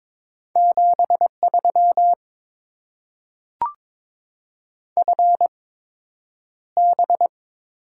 Morse code, heard as a keyed tone, about 22 wpm, spaced as for 9 wpm, 700 Hz. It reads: 73 FB
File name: Top 20 QSO Elements - 8x Character Spacing - 22wpm